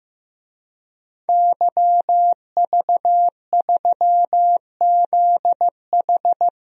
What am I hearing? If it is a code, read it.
Morse code, heard as a keyed tone, about 15 wpm, 700 Hz. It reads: YV3ZH
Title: DXCC-100 Call Signs - 5 Characters - 15wpm